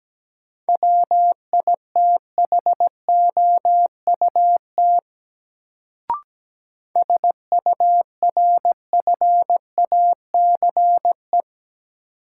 Morse code, heard as a keyed tone, about 17 wpm, 700 Hz. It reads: WITHOUT SURFACE
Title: Sentences from Top 600 Words - New Word Drill - 17wpm